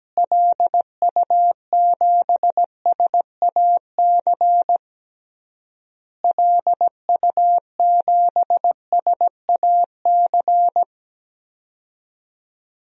Morse code, heard as a keyed tone, about 17 wpm, 700 Hz. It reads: LU7SAC LU7SAC
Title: DXCC-100 Call Signs - 6 Characters - Repeated 3x - 17wpm